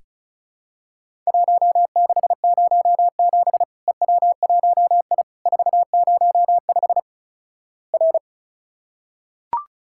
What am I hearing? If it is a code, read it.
Morse code, heard as a keyed tone, about 36 wpm, 700 Hz. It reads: 1607 EW1I 405 R
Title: Worked All Europe DX Contest - QTC Exchange - 35wpm